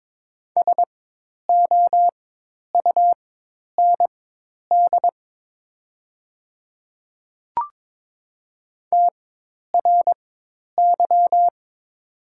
Morse code, heard as a keyed tone, about 22 wpm, spaced as for 10 wpm, 700 Hz. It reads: SOUND TRY
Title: 4x Character Spacing - Top 200 Words - 22wpm